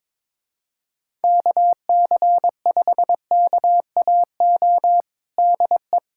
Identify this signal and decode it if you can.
Morse code, heard as a keyed tone, about 22 wpm, 700 Hz. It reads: KC5KAO DE
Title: USA Callsign DE Callsign 22wpm